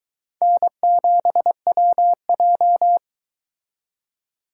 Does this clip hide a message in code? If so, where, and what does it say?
Yes: Morse code from 0.4 to 3.0 s, heard as a keyed tone, about 22 wpm, 700 Hz. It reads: N7WJ